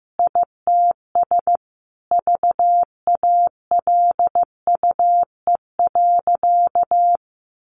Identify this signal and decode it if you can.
Morse code, heard as a keyed tone, about 15 wpm, 700 Hz. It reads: ITS VALUE.
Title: Sentences from Top 1300 Words 15wpm